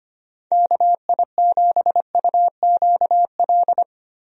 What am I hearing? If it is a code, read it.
Morse code, heard as a keyed tone, about 25 wpm, 700 Hz. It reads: KI7UQL
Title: USA Call Signs - 6 Characters - Repeated 2x - 25wpm